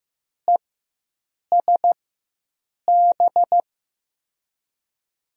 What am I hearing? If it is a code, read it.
Morse code, heard as a keyed tone, about 15 wpm, spaced as for 7 wpm, 700 Hz. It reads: ESB
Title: USA Call Signs - 6 Characters - 4x Character Spacing - 15wpm